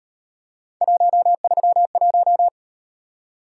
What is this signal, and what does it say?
Morse code, heard as a keyed tone, about 38 wpm, 700 Hz. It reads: 131